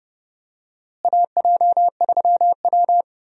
Morse code, heard as a keyed tone, about 30 wpm, 700 Hz. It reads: AJ3W
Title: USA Call Signs - 4 Characters - Repeated 2x - 30wpm